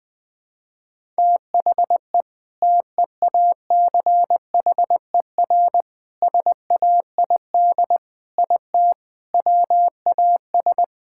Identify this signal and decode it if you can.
Morse code, heard as a keyed tone, about 20 wpm, 700 Hz. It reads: THE TEACHER SAID IT WAS
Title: All-Age Appropriate Jokes - 20wpm